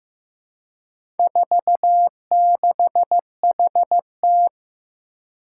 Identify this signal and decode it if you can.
Morse code, heard as a keyed tone, about 15 wpm, 700 Hz. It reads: 46HT
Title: Four Challenging Letter-Number Combinations 15wpm